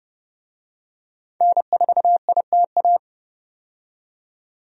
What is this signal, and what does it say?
Morse code, heard as a keyed tone, about 30 wpm, 700 Hz. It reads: N4ITA
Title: USA Call Signs - 5 Characters - 30wpm